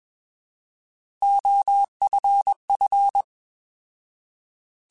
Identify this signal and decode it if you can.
Morse code, heard as a keyed tone, about 21 wpm, 800 Hz. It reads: OFF
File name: Alphabetical / Alphabetical mp3s / Os @20